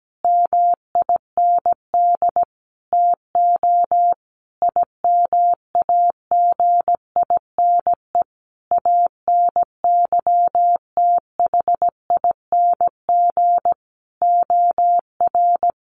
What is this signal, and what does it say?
Morse code, heard as a keyed tone, about 17 wpm, 700 Hz. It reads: MIND TO IMAGINE ANYTHING OR
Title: Sentences from Top 1400 Words 17wpm